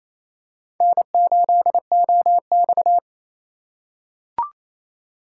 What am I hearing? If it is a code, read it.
Morse code, heard as a keyed tone, about 28 wpm, 700 Hz. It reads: N8OX
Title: USA Call Signs - 4 Characters - 3x Character Spacing - 28wpm